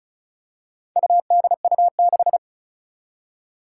Morse code, heard as a keyed tone, about 36 wpm, 700 Hz. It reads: UDU6
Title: Four Challenging Letter-Number Combinations 35wpm